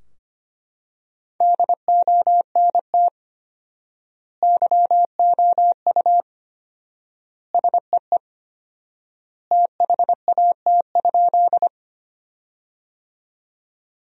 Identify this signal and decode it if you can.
Morse code, heard as a keyed tone, about 25 wpm, 700 Hz. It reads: DONT YOU SEE THAT?
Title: 4x Word-Spacing - Sentences from Top 300 Words - 25wpm